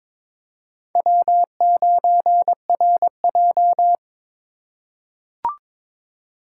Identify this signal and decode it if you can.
Morse code, heard as a keyed tone, about 22 wpm, 700 Hz. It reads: W9RJ
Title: USA Call Signs - 3x Character Spacing - 22wpm